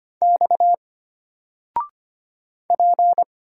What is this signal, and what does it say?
Morse code, heard as a keyed tone, about 25 wpm, 700 Hz. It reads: X P
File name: Single-Letter-Number-25wpm